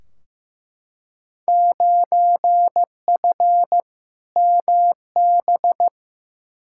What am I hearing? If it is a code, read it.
Morse code, heard as a keyed tone, about 15 wpm, 700 Hz. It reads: 9F MB
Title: Field Day Exchanges Encore - 15wpm